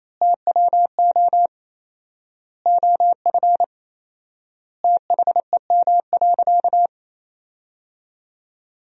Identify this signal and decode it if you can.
Morse code, heard as a keyed tone, about 28 wpm, 700 Hz. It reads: TWO OF THEM.
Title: 4x Word-Spacing - Sentences from Top 100 Words - 28wpm